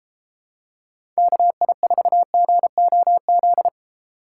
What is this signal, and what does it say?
Morse code, heard as a keyed tone, about 30 wpm, 700 Hz. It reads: KI4GOZ